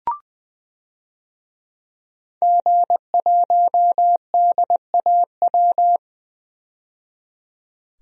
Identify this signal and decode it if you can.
Morse code, heard as a keyed tone, about 20 wpm, 700 Hz. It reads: G1DAW
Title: DXCC-100 Call Signs - 5 Characters - 20wpm